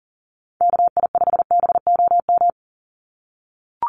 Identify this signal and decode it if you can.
Morse code, heard as a keyed tone, about 40 wpm, 700 Hz. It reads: KI5BOM
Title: USA Call Signs 40wpm